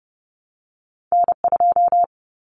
Morse code, heard as a keyed tone, about 30 wpm, 700 Hz. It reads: N2
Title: Common Cut Numbers - 2-Digit 30wpm